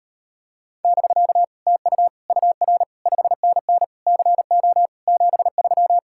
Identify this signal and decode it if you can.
Morse code, heard as a keyed tone, about 33 wpm, 700 Hz. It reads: <BK> TU UR 5NN CO 73